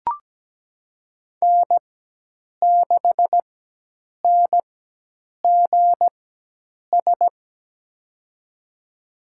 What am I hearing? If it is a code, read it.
Morse code, heard as a keyed tone, about 17 wpm, spaced as for 8 wpm, 700 Hz. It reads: N6NGS
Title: USA Call Signs - 5 Characters - 4x Character Spacing - 17wpm